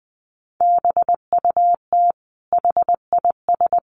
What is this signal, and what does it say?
Morse code, heard as a keyed tone, about 20 wpm, 700 Hz. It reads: BUT HIS
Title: Sentences from Top 800 Words 20wpm